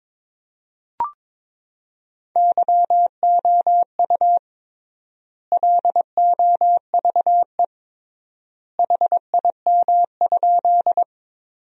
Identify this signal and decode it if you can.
Morse code, heard as a keyed tone, about 22 wpm, 700 Hz. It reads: YOU LOVE HIM?
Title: 3x Word-Spacing - Sentences from Top 400 Words - 22wpm